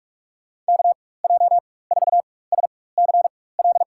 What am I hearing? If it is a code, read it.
Morse code, heard as a keyed tone, about 41 wpm, spaced as for 20 wpm, 700 Hz. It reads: KJ4SCL
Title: USA Call Signs - 6 Characters - 4x Character Spacing - 45wpm